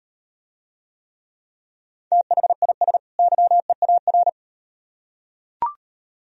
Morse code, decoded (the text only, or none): THIS YEAR